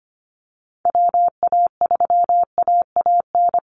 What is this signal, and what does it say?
Morse code, heard as a keyed tone, about 25 wpm, 700 Hz. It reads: WA3AAN
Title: Call Signs Repeated 2x - 25wpm